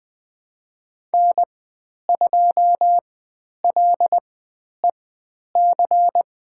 Morse code, heard as a keyed tone, about 20 wpm, spaced as for 10 wpm, 700 Hz. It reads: N2LEC
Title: 095 Lesson 16g - Letter c - Call Signs 20wpm